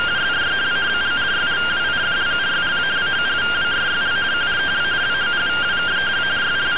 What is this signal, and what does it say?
Two-tone teleprinter signal, about 170 Hz shift.